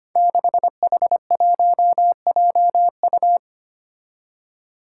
Morse code, heard as a keyed tone, about 25 wpm, 700 Hz. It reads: 6H1JU
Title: Five Challenging Letter-Number Combinations 25wpm